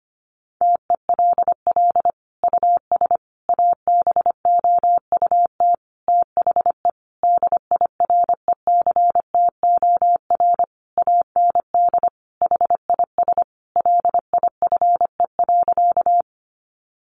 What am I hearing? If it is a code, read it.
Morse code, heard as a keyed tone, about 25 wpm, 700 Hz. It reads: TELL US ABOUT THE DIRECTOR AND HIS LIFE.
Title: Sentences from Top 1300 Words 25wpm